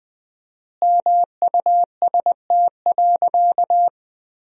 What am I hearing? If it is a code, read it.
Morse code, heard as a keyed tone, about 20 wpm, 700 Hz. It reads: MUST.
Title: 4x Word-Spacing - Sentences from Top 200 Words - 20wpm